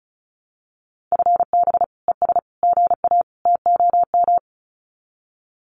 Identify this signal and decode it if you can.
Morse code, heard as a keyed tone, about 34 wpm, 700 Hz. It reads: FB ES GA TOM
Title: Common QSO Phrases Encore 35wpm